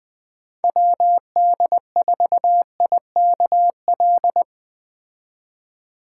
Morse code, heard as a keyed tone, about 20 wpm, 700 Hz. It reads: WD4IKL
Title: Speed Racing - USA 2x3 Call Signs 17wpm